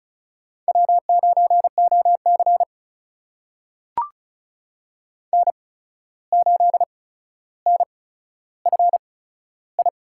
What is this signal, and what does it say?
Morse code, heard as a keyed tone, about 36 wpm, spaced as for 14 wpm, 700 Hz. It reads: W9OC N8NFI
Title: USA Call Signs - 8x Character Spacing - 35wpm